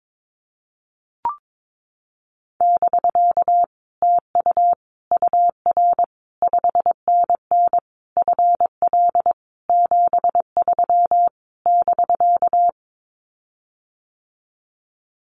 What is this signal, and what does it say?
Morse code, heard as a keyed tone, about 22 wpm, 700 Hz. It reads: <BK> TU UR 5NN FL 73 <BK>
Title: Common QSO Phrases Encore 22wpm